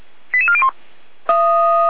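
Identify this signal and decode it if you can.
Five-tone selective call (ZVEI), heard as quick runs of five tones, tones of about 80 ms. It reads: 88401
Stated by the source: ZVEI five-tone call 88402 received on an SDR, archived by feuerwehr-orlamuende.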